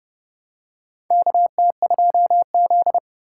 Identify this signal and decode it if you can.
Morse code, heard as a keyed tone, about 30 wpm, 700 Hz. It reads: KT2Z
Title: USA Call Signs - 4 Characters - 4x Character Spacing - 30wpm